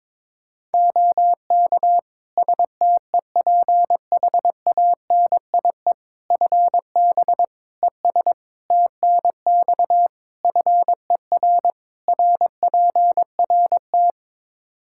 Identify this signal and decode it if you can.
Morse code, heard as a keyed tone, about 22 wpm, 700 Hz. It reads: OK STEPHANIE FB ES TNX FER RPRT
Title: Practice QSO - Exchange 2 - Info - 22wpm